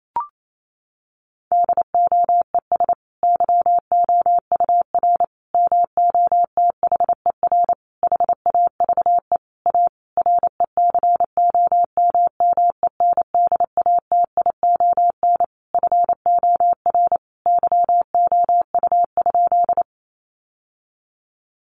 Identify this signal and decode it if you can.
Morse code, heard as a keyed tone, about 28 wpm, 700 Hz. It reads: DOES YOUR MOTHER HAVE A RECOMMENDATION FOR YOU?
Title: Sentences from Top 1400 Words 28wpm